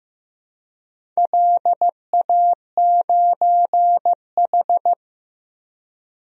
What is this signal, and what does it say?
Morse code, heard as a keyed tone, about 15 wpm, 700 Hz. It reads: LA9H
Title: DXCC-100 Call Signs - 4 Characters - 15wpm